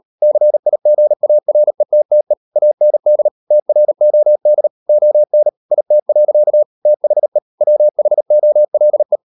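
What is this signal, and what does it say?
Morse code, heard as a keyed tone, about 38 wpm, 600 Hz. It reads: CIGARETTE AND TROD ON IT. THE WHOLE
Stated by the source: Morse code practice at 38 WPM, sending Ken Follett's novel Never in English, from CW med Gnister.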